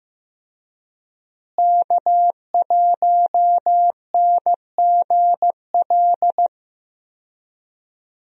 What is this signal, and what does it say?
Morse code, heard as a keyed tone, about 15 wpm, 700 Hz. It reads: K1NGL
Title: USA Call Signs - 5 Characters - Repeated 3x - 15wpm